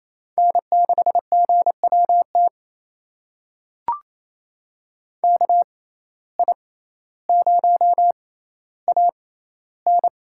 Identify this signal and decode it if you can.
Morse code, heard as a keyed tone, about 28 wpm, spaced as for 14 wpm, 700 Hz. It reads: N6GWT KI0AN
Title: USA Call Signs - 5 Characters - 6x Character Spacing - 28wpm